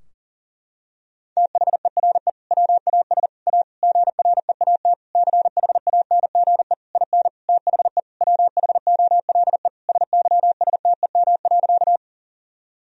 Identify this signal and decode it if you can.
Morse code, heard as a keyed tone, about 37 wpm, 700 Hz. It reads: THERE WAS A GREAT CHANGE IN THE WHOLE SYSTEM.
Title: Sentences from Top 1000 Words - Review Part 3 - 40wpm